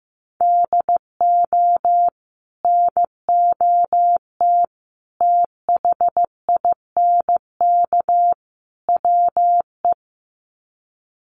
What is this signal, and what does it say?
Morse code, heard as a keyed tone, about 15 wpm, 700 Hz. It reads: DO NOT THINK WE
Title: Sets of 4 Words 15wpm